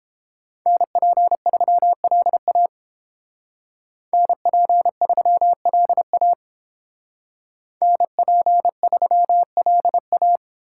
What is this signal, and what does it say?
Morse code, heard as a keyed tone, about 29 wpm, 700 Hz. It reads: NP3LA NP3LA NP3LA